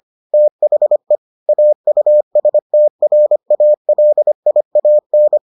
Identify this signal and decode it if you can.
Morse code, heard as a keyed tone, about 25 wpm, 600 Hz. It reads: THE AUSTRALIAN